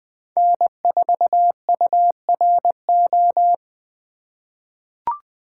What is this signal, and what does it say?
Morse code, heard as a keyed tone, about 20 wpm, 700 Hz. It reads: N4URO
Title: USA Call Signs - 5 Characters - 4x Character Spacing - 20wpm